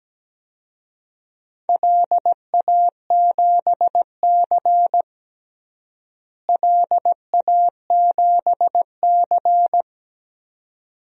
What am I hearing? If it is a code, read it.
Morse code, heard as a keyed tone, about 17 wpm, 700 Hz. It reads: LA7C LA7C
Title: DXCC-100 Call Signs - 4 Characters - Repeated 2x - 17wpm